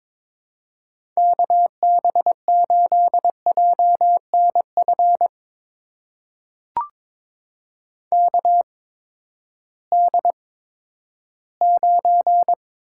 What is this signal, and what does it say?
Morse code, heard as a keyed tone, about 22 wpm, spaced as for 13 wpm, 700 Hz. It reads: KB8JNF KD9
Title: USA Call Signs - 6 Characters - 8x Character Spacing - 22wpm